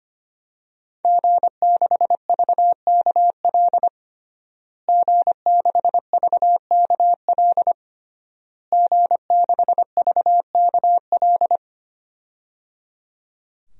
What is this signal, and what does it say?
Morse code, heard as a keyed tone, about 25 wpm, 700 Hz. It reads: G6VKL G6VKL G6VKL
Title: DXCC-100 Call Signs - 5 Characters - Repeated 3x - 25wpm